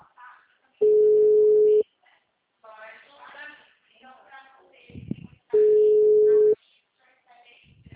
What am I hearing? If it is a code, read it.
Ringback tone (Europe, 425 Hz), with speech.